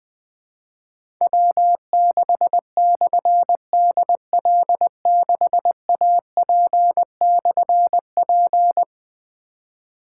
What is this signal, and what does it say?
Morse code, heard as a keyed tone, about 20 wpm, 700 Hz. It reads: W6/DL6AP/P
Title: Difficult Call Signs 20wpm